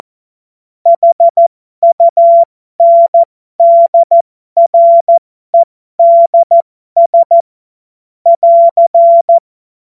Morse code, heard as a keyed tone, about 14 wpm, spaced as for 12 wpm, 680 Hz. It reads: HUNDREDS <AR>
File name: Morsekurs / Witze